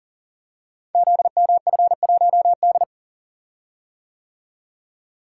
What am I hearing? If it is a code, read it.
Morse code, heard as a keyed tone, about 40 wpm, 700 Hz. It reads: ZMF1D
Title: Code Groups - 5 Letter:Numbers - 40wpm